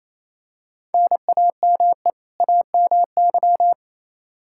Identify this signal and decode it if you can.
Morse code, heard as a keyed tone, about 28 wpm, 700 Hz. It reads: NAME AMY
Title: Practice QSO - Exchange 1 - Beginning - 28wpm